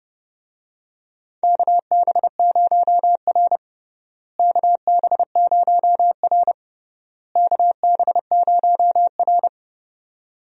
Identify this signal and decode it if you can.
Morse code, heard as a keyed tone, about 32 wpm, 700 Hz. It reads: KB0R KB0R KB0R
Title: USA Call Signs - 4 Characters - Repeated 3x - 30wpm